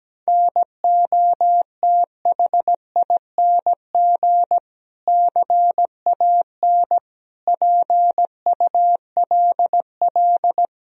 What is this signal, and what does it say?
Morse code, heard as a keyed tone, about 17 wpm, 700 Hz. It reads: NOTHING CAN PULL
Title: Sentences from Top 1000 Words - Review Part 2 - 17wpm